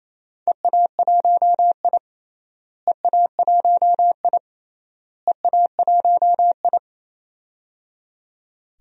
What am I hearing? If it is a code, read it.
Morse code, heard as a keyed tone, about 28 wpm, 700 Hz. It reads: EA1I EA1I EA1I
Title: DXCC-100 Call Signs - 4 Characters - Repeated 3x - 28wpm